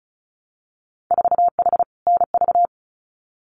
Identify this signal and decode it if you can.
Morse code, heard as a keyed tone, about 36 wpm, 700 Hz. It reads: 4H NV
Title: Winter Field Day Exchanges Encore 35wpm